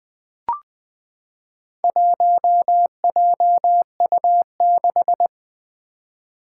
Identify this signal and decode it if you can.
Morse code, heard as a keyed tone, about 20 wpm, 700 Hz. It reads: 1JU6
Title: Four Challenging Letter-Number Combinations 20wpm